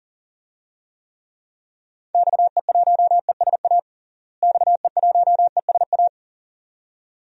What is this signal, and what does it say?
Morse code, heard as a keyed tone, about 40 wpm, 700 Hz. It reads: XE1ESA XE1ESA
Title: DXCC-100 Call Signs - 6 Characters - Repeated 2x - 40wpm